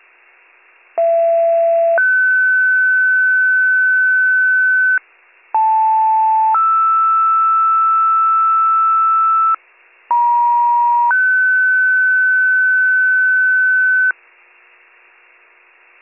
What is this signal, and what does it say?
Two-tone page, heard as a one-second tone, then a three-second tone, about 670 then 1500 Hz, about 880 then 1320 Hz, about 950 then 1500 Hz. It is heard from 1.0 to 14.1 s.